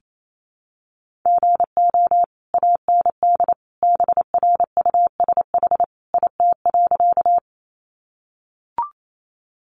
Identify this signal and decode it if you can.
Morse code, heard as a keyed tone, about 28 wpm, 700 Hz. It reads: GO AND BRUSH IT.
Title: Sentences from Top 1100 Words 28wpm